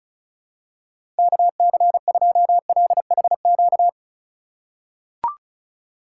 Morse code, heard as a keyed tone, about 36 wpm, 700 Hz. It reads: KC2LHQ
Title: USA Call Signs - 6 Characters - 3x Character Spacing - 35wpm